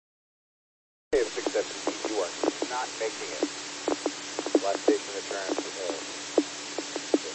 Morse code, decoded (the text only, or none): none